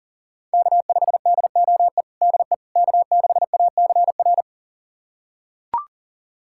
Morse code, heard as a keyed tone, about 37 wpm, 700 Hz. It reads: K5DOE DE K6ACR
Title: USA Callsign DE Callsign - 5 Characters - 40wpm